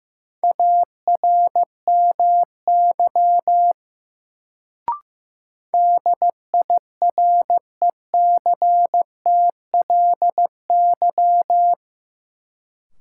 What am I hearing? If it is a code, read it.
Morse code, heard as a keyed tone, about 15 wpm, 700 Hz. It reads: ARMY DIRECTLY
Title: Sentences from Top 800 Words - New Word Drill - 15wpm